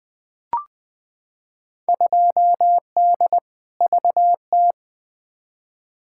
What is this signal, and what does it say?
Morse code, heard as a keyed tone, about 20 wpm, 700 Hz. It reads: 2D VT
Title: Field Day Exchanges Encore - 20wpm